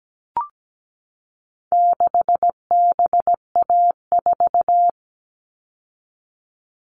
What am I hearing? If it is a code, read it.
Morse code, heard as a keyed tone, about 17 wpm, 700 Hz. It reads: 6BA4
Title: All Cut Numbers - 4-Digit 17wpm